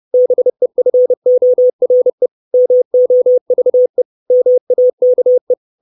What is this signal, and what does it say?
Morse code, heard as a keyed tone, about 30 wpm, 500 Hz. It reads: BEFORE MOVE MAKE
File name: 632 Words 30WPM